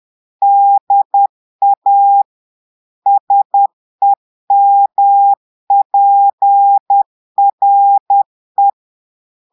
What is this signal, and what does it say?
Morse code, heard as a keyed tone, about 10 wpm, 800 Hz. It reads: DA SEMPRE
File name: Portugais - texte en clair 1 - vitesse - 10 mots minute